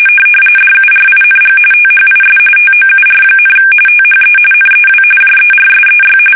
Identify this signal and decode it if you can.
RTTY, about 45.45 baud. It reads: DURING THIS TOUR, THE NEW JERSEY
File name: fsk850 / battleshipnj